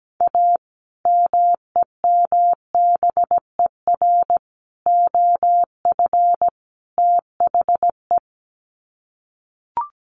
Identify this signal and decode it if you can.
Morse code, heard as a keyed tone, about 17 wpm, 700 Hz. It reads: A MEMBER OF THE
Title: Sets of 4 Words 17wpm